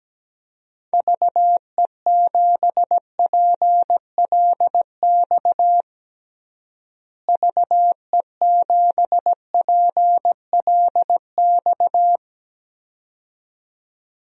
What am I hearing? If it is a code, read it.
Morse code, heard as a keyed tone, about 17 wpm, 700 Hz. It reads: VE7PLX VE7PLX
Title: DXCC-100 Call Signs - 6 Characters - Repeated 2x - 17wpm